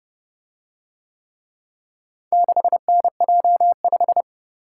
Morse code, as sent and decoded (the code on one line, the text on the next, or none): -.... -. .--- .....
6NJ5